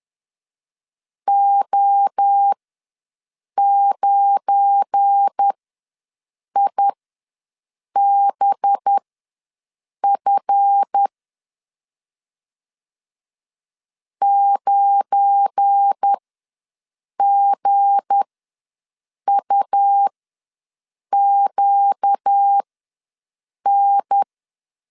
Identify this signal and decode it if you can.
Morse code, heard as a keyed tone, about 10 wpm, spaced as for 6 wpm, 800 Hz. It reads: O9IBF 9GUQN
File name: LEZIONI / 7 GF9Z / 7 GF9 ws 10wpm 4034